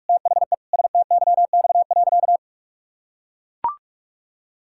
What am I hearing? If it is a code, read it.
Morse code, heard as a keyed tone, about 41 wpm, 700 Hz. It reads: THE STYX.
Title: Pangram Challenge v2 - 45wpm